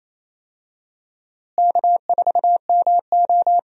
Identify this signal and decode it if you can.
Morse code, heard as a keyed tone, about 28 wpm, 700 Hz. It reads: K4MO
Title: USA Call Signs - 4 Characters - Repeated 3x - 28wpm